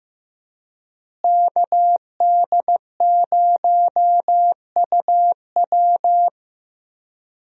Morse code, heard as a keyed tone, about 15 wpm, 700 Hz. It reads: KD0UW
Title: USA Call Signs - 5 Characters - Repeated 2x - 15wpm